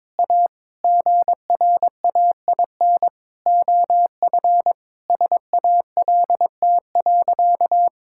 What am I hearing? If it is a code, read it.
Morse code, heard as a keyed tone, about 22 wpm, 700 Hz. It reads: A GRAIN OF SALT.
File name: Sentences with 7-Word Phrases - 22wpm